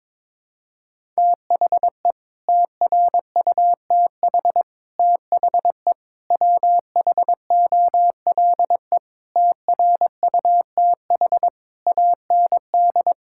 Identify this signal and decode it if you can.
Morse code, heard as a keyed tone, about 22 wpm, 700 Hz. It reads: THE TRUTH THE WHOLE TRUTH AND
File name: Sets of 9 Words - 22wpm